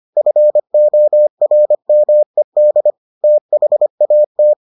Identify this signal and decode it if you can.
Morse code, heard as a keyed tone, about 25 wpm, 600 Hz. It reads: FORMED THAT